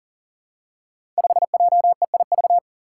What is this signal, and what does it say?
Morse code, heard as a keyed tone, about 42 wpm, 700 Hz. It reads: 5JEIV
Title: Five Challenging Letter-Number Combinations 40wpm